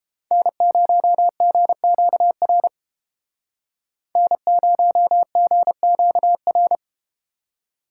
Morse code, heard as a keyed tone, about 32 wpm, 700 Hz. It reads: N0GQR N0GQR